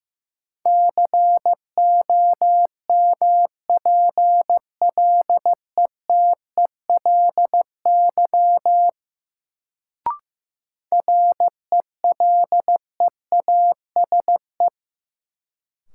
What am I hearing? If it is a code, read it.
Morse code, heard as a keyed tone, about 15 wpm, 700 Hz. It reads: COMPLETELY RELEASE
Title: Sentences from Top 1100 Words - New Word Drill Alt - 15wpm